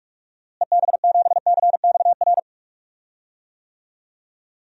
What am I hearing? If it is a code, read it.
Morse code, heard as a keyed tone, about 46 wpm, 700 Hz. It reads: EB7CXR
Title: DXCC-100 Call Signs - 6 Characters - Repeated 3x - 45wpm